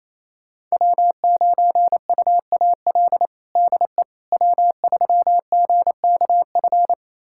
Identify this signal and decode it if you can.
Morse code, heard as a keyed tone, about 28 wpm, 700 Hz. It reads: W9UAL DE W3GKF
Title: USA Callsign DE Callsign - 5 Characters - 28wpm